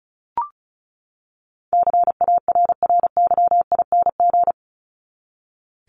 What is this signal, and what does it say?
Morse code, heard as a keyed tone, about 34 wpm, 700 Hz. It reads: CARRYING
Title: ING Suffix - Top 100 Words with Base Words 35wpm